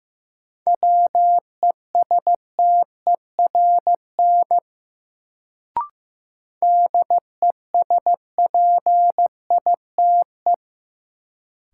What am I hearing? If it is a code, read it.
Morse code, heard as a keyed tone, about 15 wpm, 700 Hz. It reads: WESTERN DESPITE
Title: Sentences from Top 1000 Words - Review Part 7 - New Word Drill - 15wpm